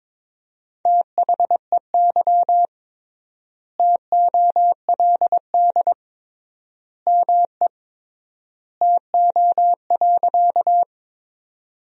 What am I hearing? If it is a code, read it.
Morse code, heard as a keyed tone, about 22 wpm, 700 Hz. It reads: THEY TOLD ME TO.
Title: 3x Word-Spacing - Sentences from Top 400 Words - 22wpm